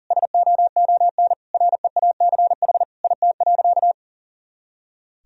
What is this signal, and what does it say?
Morse code, heard as a keyed tone, about 37 wpm, 700 Hz. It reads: SOON REACH IT.
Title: Sentences from Top 400 Words and Abbreviations 40wpm